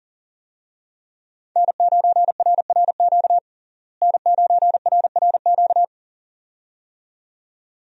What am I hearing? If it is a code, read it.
Morse code, heard as a keyed tone, about 40 wpm, 700 Hz. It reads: N9RRQ N9RRQ
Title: USA Call Signs - 5 Characters - Repeated 2x - 40wpm